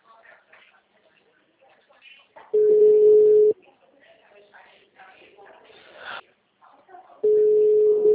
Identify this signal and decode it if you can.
Ringback tone (Europe, 425 Hz), with speech.